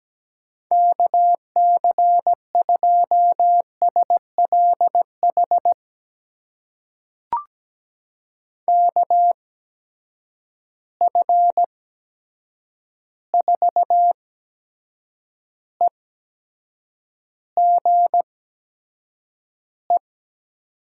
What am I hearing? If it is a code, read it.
Morse code, heard as a keyed tone, about 17 wpm, spaced as for 7 wpm, 700 Hz. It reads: KC2SLH KF4EGE
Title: USA Call Signs - 6 Characters - 8x Character Spacing - 17wpm